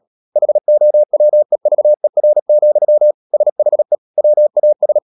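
Morse code, heard as a keyed tone, about 37 wpm, 600 Hz. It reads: HOWEVER, SHE WAS
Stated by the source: Morse code practice at 35–39 wpm sending English text of Ken Follett's novel Never, from CW med Gnister.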